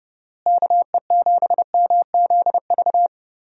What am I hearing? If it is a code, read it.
Morse code, heard as a keyed tone, about 30 wpm, 700 Hz. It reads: KE7MZV
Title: USA Call Signs - 6 Characters - Repeated 2x - 30wpm